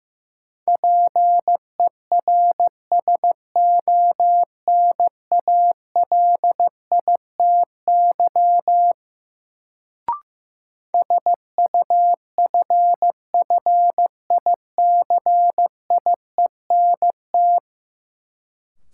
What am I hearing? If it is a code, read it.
Morse code, heard as a keyed tone, about 15 wpm, 700 Hz. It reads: PERSONALITY SUFFICIENT
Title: Sentences from Top 1400 Words - New Word Drill Alt - 15wpm